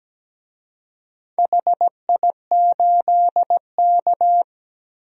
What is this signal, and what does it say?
Morse code, heard as a keyed tone, about 17 wpm, 700 Hz. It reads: HI8K
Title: DXCC-100 Call Signs - 4 Characters - 17wpm